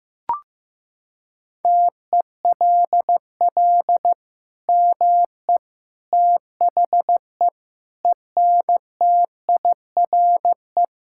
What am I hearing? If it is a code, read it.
Morse code, heard as a keyed tone, about 15 wpm, 700 Hz. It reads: TELL ME THE ENTIRE
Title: Sentences from Top 1000 Words - Review Part 5 - 15wpm